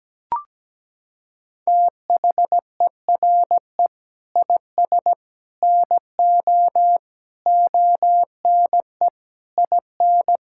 Morse code, heard as a keyed tone, about 17 wpm, 700 Hz. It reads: THERE IS NO ONE IN
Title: Sentences from Top 1000 Words - Review Part 6 - 17wpm